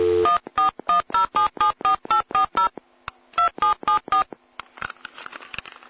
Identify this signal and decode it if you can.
DTMF, heard as telephone touch tones, tones of about 120 ms. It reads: 1410774840 2774